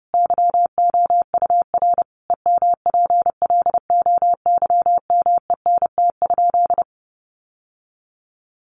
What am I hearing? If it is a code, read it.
Morse code, heard as a keyed tone, about 30 wpm, 700 Hz. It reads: YOUR EMPLOYMENT?
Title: Sentences from Top 1500 Words 30wpm